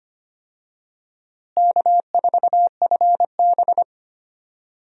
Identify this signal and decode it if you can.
Morse code, heard as a keyed tone, about 25 wpm, 700 Hz. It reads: K4FB
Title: USA Call Signs - 4 Characters - Repeated 3x - 25wpm